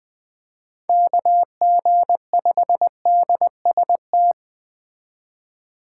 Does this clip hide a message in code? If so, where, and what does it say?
Yes: Morse code from 0.9 to 4.3 s, heard as a keyed tone, about 20 wpm, 700 Hz. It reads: KG5DST